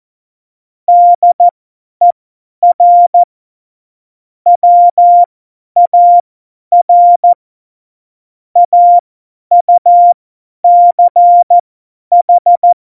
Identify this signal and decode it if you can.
Morse code, heard as a keyed tone, about 14 wpm, spaced as for 10 wpm, 700 Hz. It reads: DER WAR AUCH
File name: Morsekurs / Witze / Witz 15. August 17, Tempo 50 F70